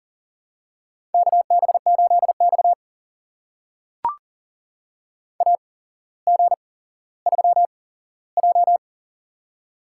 Morse code, heard as a keyed tone, about 40 wpm, spaced as for 17 wpm, 700 Hz. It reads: KB8X AG3J
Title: USA Call Signs - 4 Characters - 8x Character Spacing - 40wpm